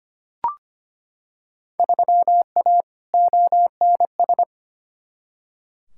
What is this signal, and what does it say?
Morse code, heard as a keyed tone, about 25 wpm, 700 Hz. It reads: 3A ONS
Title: Field Day Exchanges - 25wpm